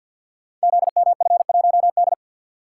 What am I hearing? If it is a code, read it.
Morse code, heard as a keyed tone, about 49 wpm, 700 Hz. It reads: ZMF1D
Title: Code Groups - 5 Letter:Numbers - 50wpm